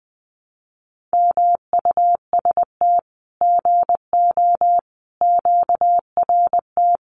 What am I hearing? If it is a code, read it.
Morse code, heard as a keyed tone, about 20 wpm, 700 Hz. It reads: MUST GO QRT
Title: Common QSO Phrases 20wpm